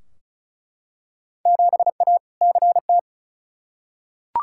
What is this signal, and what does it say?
Morse code, heard as a keyed tone, about 36 wpm, 700 Hz. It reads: 7A CT
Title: Field Day Exchanges - 35wpm